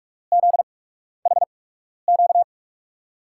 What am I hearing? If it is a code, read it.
Morse code, heard as a keyed tone, about 46 wpm, spaced as for 12 wpm, 700 Hz. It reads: ZHQ
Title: USA Call Signs - 6 Characters - 8x Character Spacing - 45wpm